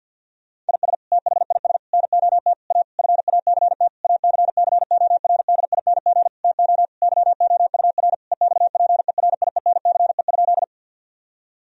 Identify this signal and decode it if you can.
Morse code, heard as a keyed tone, about 50 wpm, 700 Hz. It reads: IS THIS NOT A FACT ACCORDING TO YOUR EXPERIENCE?
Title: Sentences from Top 1000 Words - Review Part 4 - 50wpm